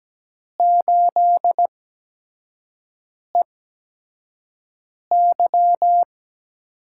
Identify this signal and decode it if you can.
Morse code, heard as a keyed tone, about 17 wpm, spaced as for 5 wpm, 700 Hz. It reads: 8EY
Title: USA Call Signs - 4 Characters - 8x Character Spacing - 17wpm